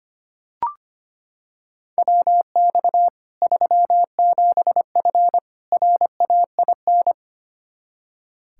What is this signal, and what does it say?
Morse code, heard as a keyed tone, about 25 wpm, 700 Hz. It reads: WX 37F RAIN
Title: Practice QSO - Exchange 2 - Info - 25wpm